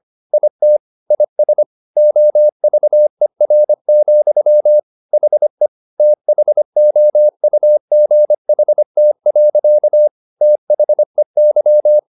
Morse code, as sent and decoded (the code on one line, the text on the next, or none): .. - / .. ... / --- ...- . .-. --..-- / .... / - .... --- ..- --. .... - .-.-.- / - .... -.--
IT IS OVER, HE THOUGHT. THEY